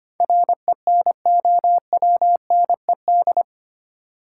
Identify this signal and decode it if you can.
Morse code, heard as a keyed tone, about 25 wpm, 700 Hz. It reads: RENOWNED
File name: Eight Letter Words - 25wpm